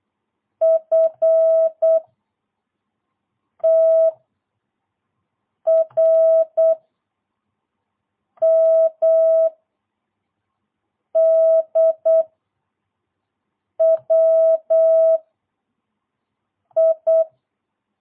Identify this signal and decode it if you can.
Morse code, heard as a keyed tone, about 8 wpm, 650 Hz. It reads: F T R M D W I
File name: jh8cbh / 2022cw